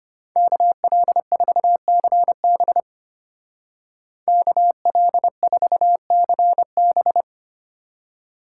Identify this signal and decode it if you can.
Morse code, heard as a keyed tone, about 32 wpm, 700 Hz. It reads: KL4CB KL4CB